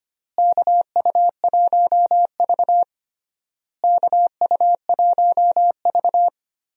Morse code, heard as a keyed tone, about 25 wpm, 700 Hz. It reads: KU1V KU1V
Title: USA Call Signs - 4 Characters - Repeated 3x - 25wpm